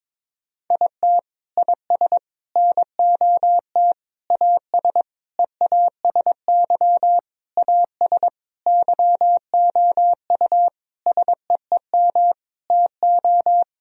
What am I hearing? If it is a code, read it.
Morse code, heard as a keyed tone, about 22 wpm, 700 Hz. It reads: IT IS NOT AS EASY AS YOU SEEM TO
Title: Sentences from Top 1000 Words - Review Part 1 - 22wpm